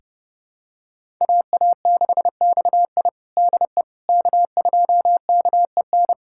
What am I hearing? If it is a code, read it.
Morse code, heard as a keyed tone, about 32 wpm, 700 Hz. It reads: AA6XI DE K2KEN
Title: USA Callsign DE Callsign - 5 Characters - 30wpm